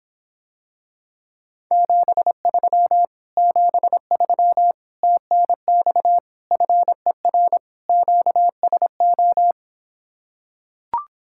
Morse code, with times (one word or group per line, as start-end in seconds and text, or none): 1.7-3.1 73
3.4-4.7 73
5.0-6.2 TNX
6.5-7.6 FER
7.9-9.5 QSO